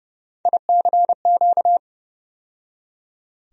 Morse code, heard as a keyed tone, about 29 wpm, 700 Hz. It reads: ICQ
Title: USA Call Sign Postfixes - 3 Letters 30wpm